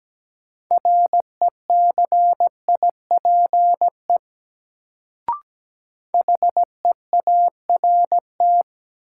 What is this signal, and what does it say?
Morse code, heard as a keyed tone, about 17 wpm, 700 Hz. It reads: RECIPE HEART
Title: Sentences from Top 1200 Words - New Word Drill - 17wpm